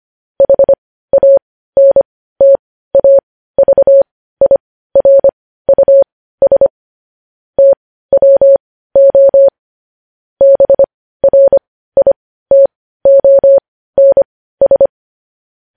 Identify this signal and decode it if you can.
Morse code, heard as a keyed tone, about 25 wpm, spaced as for 15 wpm, 550 Hz. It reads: HANTAVIRUS TWO BRITONS
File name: podcast / News headlines-1x-25x15